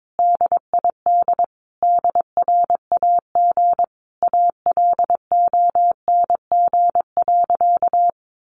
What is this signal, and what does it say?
Morse code, heard as a keyed tone, about 22 wpm, 700 Hz. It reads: DID DRAG ALONG.
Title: Sentences from Top 1500 Words 22wpm